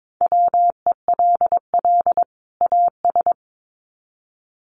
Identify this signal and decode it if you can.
Morse code, heard as a keyed tone, about 22 wpm, 700 Hz. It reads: WELL AS
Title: Sets of 2 Words 22wpm